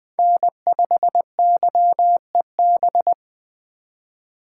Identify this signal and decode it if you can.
Morse code, heard as a keyed tone, about 20 wpm, 700 Hz. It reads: N5YEB
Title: USA Call Signs - 5 Characters - 6x Character Spacing - 20wpm